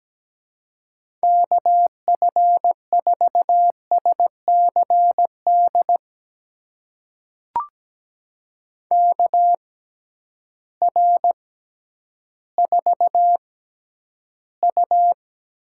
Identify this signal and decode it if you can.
Morse code, heard as a keyed tone, about 17 wpm, spaced as for 10 wpm, 700 Hz. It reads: KF4SCD KR4U
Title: USA Call Signs - 6x Character Spacing - 17wpm